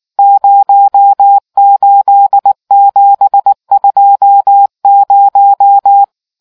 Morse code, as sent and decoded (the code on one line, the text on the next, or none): ----- ---.. --... ..--- -----
08720